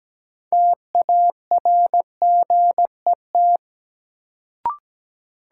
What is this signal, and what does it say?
Morse code, heard as a keyed tone, about 17 wpm, 700 Hz. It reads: TARGET
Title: Sentences from Top 1200 Words - New Word Drill Alt - 17wpm